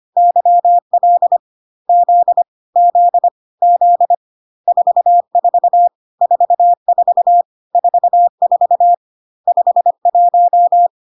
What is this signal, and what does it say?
Morse code, heard as a keyed tone, about 25 wpm, 700 Hz. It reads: YL Z Z Z 44 44 44 51
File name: cw / CW common abbreviations alphabetical repeated 3 times 25 25wpm ik1hge